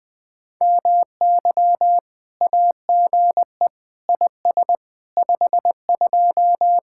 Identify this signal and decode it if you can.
Morse code, heard as a keyed tone, about 20 wpm, 700 Hz. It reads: MY AGE IS 52
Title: Speed-Racing Common QSO Phrases Encore - 17wpm